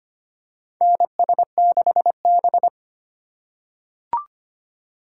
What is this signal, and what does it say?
Morse code, heard as a keyed tone, about 25 wpm, 700 Hz. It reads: NS6B
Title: USA Call Signs - 4 Characters - 3x Character Spacing - 25wpm